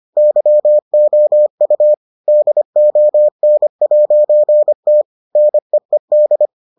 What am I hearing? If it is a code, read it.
Morse code, heard as a keyed tone, about 25 wpm, 600 Hz. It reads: YOU DON'T NEED